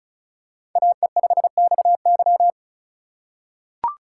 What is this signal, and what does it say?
Morse code, heard as a keyed tone, about 36 wpm, 700 Hz. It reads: AE5XY
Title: USA Call Signs - 5 Characters - 2x Character Spacing - 35wpm